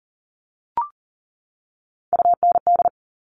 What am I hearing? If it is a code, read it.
Morse code, heard as a keyed tone, about 38 wpm, 700 Hz. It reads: UND
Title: Top Three Letter Combinations in Words 40wpm